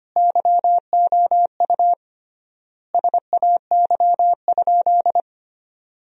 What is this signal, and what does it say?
Morse code, heard as a keyed tone, about 25 wpm, 700 Hz. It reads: YOU SAY?
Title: 3x Word-Spacing - Sentences from Top 200 Words - 25wpm